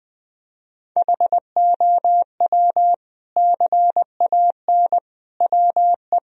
Morse code, heard as a keyed tone, about 20 wpm, 700 Hz. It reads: HOW CAN WE
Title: Sentences from Top 1000 Words - Review Part 2 - 20wpm